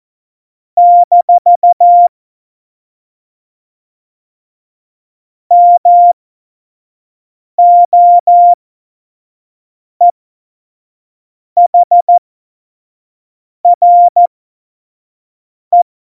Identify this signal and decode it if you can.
Morse code, heard as a keyed tone, about 14 wpm, spaced as for 5 wpm, 700 Hz. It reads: - MOEHRE
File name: Morsekurs / Witze / Witz 24. Oktober 17, Tempo 25